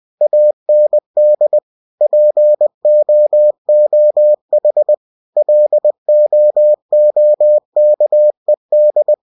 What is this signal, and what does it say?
Morse code, heard as a keyed tone, about 20 wpm, 600 Hz. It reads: AND POOH LOOKED
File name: cw-books / winnie-the-pooh / winnie-20wpm-page-0031